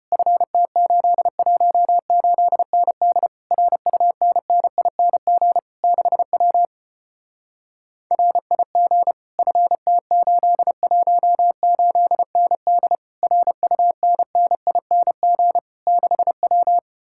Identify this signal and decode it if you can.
Morse code, heard as a keyed tone, about 33 wpm, 700 Hz. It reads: FT818ND RUNNING 6W RIG FT818ND RUNNING 6W